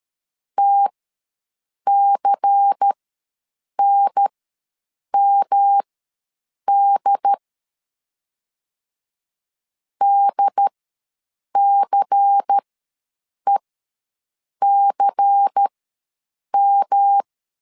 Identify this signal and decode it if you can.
Morse code, heard as a keyed tone, about 13 wpm, spaced as for 7 wpm, 800 Hz. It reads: TCNMD DCECM
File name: LEZIONI / 5 LBVC / 5 LBVC ws 12wpm 3159